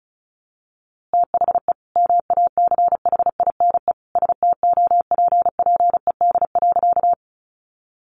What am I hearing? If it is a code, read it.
Morse code, heard as a keyed tone, about 34 wpm, 700 Hz. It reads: THE MACHINE STOPPED.
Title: A Sound Of Thunder 35wpm